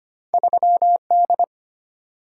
Morse code, heard as a keyed tone, about 25 wpm, 700 Hz. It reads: 3D
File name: Two Letter-Number Combinations - 25wpm